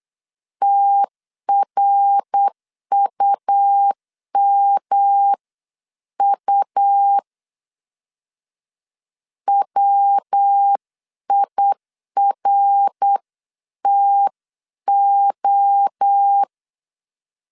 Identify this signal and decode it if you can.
Morse code, heard as a keyed tone, about 8 wpm, 800 Hz. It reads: TRUMU WIRTO